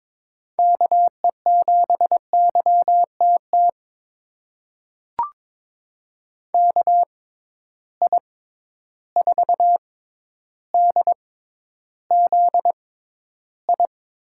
Morse code, heard as a keyed tone, about 22 wpm, spaced as for 11 wpm, 700 Hz. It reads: KE7YTT KI4DZI
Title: USA Call Signs - 6 Characters - 6x Character Spacing - 22wpm